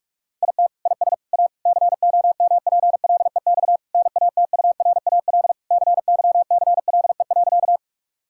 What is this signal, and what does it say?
Morse code, heard as a keyed tone, about 46 wpm, 700 Hz. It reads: IT IS A COMPLEX NATURAL CYCLE.
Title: Sentences from Top 1000 Words - Review Part 6 - 45wpm